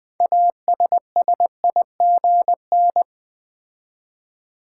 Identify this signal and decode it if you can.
Morse code, heard as a keyed tone, about 20 wpm, 700 Hz. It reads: ASSIGN
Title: 6 Letter Words - First 3 Letters Given - 20wpm